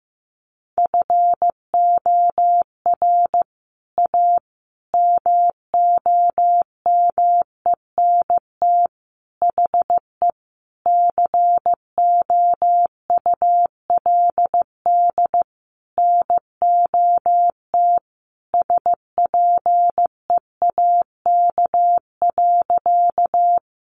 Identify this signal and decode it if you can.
Morse code, heard as a keyed tone, about 15 wpm, 700 Hz. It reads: FOR A MOMENT HE COULD NOT SPEAK.
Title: Sentences from Top 1100 Words 15wpm